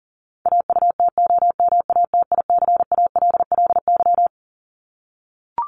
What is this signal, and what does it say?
Morse code, heard as a keyed tone, about 40 wpm, 700 Hz. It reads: AUTOMATICALLY
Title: Top 1751-2000 Words 40wpm